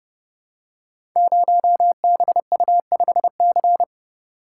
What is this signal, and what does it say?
Morse code, heard as a keyed tone, about 30 wpm, 700 Hz. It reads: 0BU5C
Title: Code Groups - 5 Letter:Numbers - 30wpm